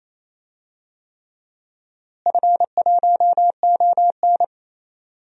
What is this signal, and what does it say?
Morse code, heard as a keyed tone, about 28 wpm, 700 Hz. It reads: F1ON
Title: DXCC-100 Call Signs - 4 Characters - Repeated 3x - 28wpm